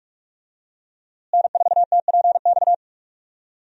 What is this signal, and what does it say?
Morse code, heard as a keyed tone, about 46 wpm, 700 Hz. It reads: N4TPX
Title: USA Call Signs - 5 Characters - Repeated 3x - 45wpm